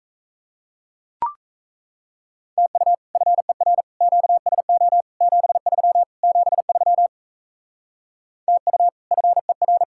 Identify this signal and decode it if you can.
Morse code, heard as a keyed tone, about 35 wpm, 700 Hz. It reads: TU FER QSO 73 73 TU FER
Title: Speed-Racing Common QSO Phrases - 28wpm